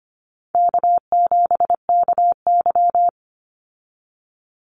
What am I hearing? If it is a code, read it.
Morse code, heard as a keyed tone, about 25 wpm, 700 Hz. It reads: K7KY
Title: Easy Call Signs - 25wpm